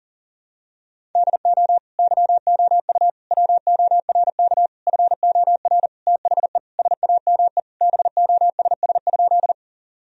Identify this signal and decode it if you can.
Morse code, heard as a keyed tone, about 40 wpm, 700 Hz. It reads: DO YOU WORK FOR THE SAME BOSS?
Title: Sentences from Top 1000 Words - Review Part 4 - 40wpm